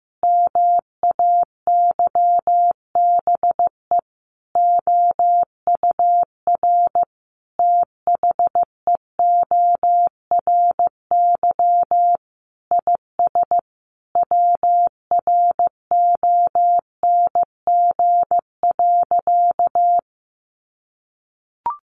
Morse code, heard as a keyed tone, about 15 wpm, 700 Hz. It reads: MAYBE OUR THEORY IS WRONG.
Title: A Sound Of Thunder 15wpm